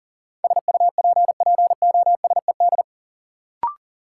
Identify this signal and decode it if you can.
Morse code, heard as a keyed tone, about 37 wpm, 700 Hz. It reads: SUPPOSED
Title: Sets of 3 Words - New Word Drill - 40wpm